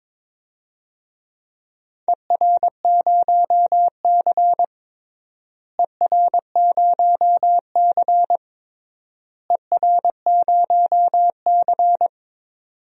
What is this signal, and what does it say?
Morse code, heard as a keyed tone, about 21 wpm, 700 Hz. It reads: ER0C ER0C ER0C
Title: DXCC-100 Call Signs - 4 Characters - Repeated 3x - 22wpm